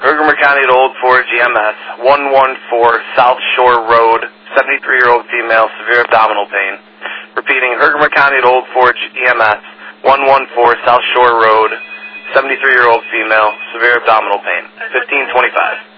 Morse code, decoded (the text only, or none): none